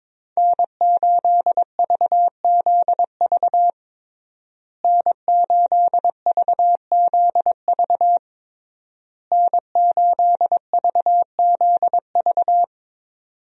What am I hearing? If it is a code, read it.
Morse code, heard as a keyed tone, about 22 wpm, 700 Hz. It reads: N8VZV N8VZV N8VZV
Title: USA Call Signs - 5 Characters - Repeated 3x - 22wpm